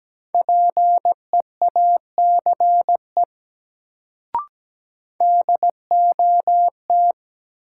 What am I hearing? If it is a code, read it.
Morse code, heard as a keyed tone, about 17 wpm, 700 Hz. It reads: PEACE DOT
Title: Sentences from Top 1500 Words - New Word Drill Alt - 17wpm